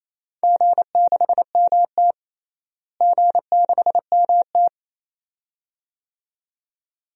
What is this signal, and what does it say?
Morse code, heard as a keyed tone, about 28 wpm, 700 Hz. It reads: G6MT G6MT
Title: DXCC-100 Call Signs - 4 Characters - Repeated 2x - 28wpm